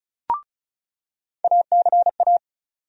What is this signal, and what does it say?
Morse code, heard as a keyed tone, about 34 wpm, 700 Hz. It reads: ACA
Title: USA Call Sign Postfixes - 3 Letters 35wpm